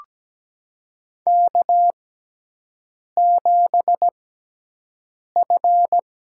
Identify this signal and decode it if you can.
Morse code, heard as a keyed tone, about 17 wpm, spaced as for 6 wpm, 700 Hz. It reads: K7F
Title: USA Call Signs - 5 Characters - 6x Character Spacing - 17wpm